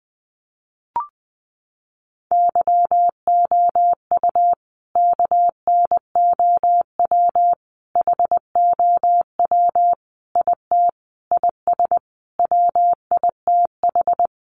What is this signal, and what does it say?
Morse code, heard as a keyed tone, about 20 wpm, 700 Hz. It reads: YOU KNOW HOW IT IS WITH
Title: Sentences from Top 100 Words 20wpm